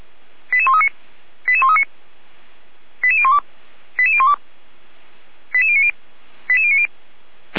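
Five-tone selective call (ZVEI), heard as quick runs of five tones, tones of about 70 ms. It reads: 88139 88139 88012 88012 88090 88090
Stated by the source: ZVEI five-tone call received by SDR, from the Feuerwehr Orlamünde audio server.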